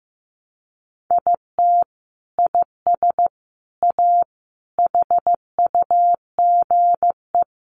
Morse code, heard as a keyed tone, about 15 wpm, 700 Hz. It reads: IT IS A HUGE
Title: Sentences from Top 1100 Words 15wpm